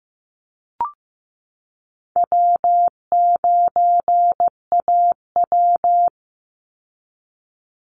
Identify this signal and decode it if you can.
Morse code, heard as a keyed tone, about 15 wpm, 700 Hz. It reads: W9AW
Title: Easy Call Signs Repeated 3x - 15wpm